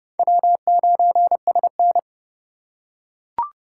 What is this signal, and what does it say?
Morse code, heard as a keyed tone, about 29 wpm, 700 Hz. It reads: W9SN
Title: USA Call Signs - 4 Characters - 2x Character Spacing - 30wpm